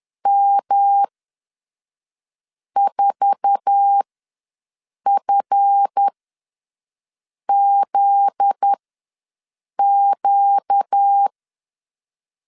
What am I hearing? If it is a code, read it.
Morse code, heard as a keyed tone, about 10 wpm, spaced as for 5 wpm, 800 Hz. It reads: M4FZQ
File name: LEZIONI / 8 YJ45 / 8 YJ4 ws 10wpm 4020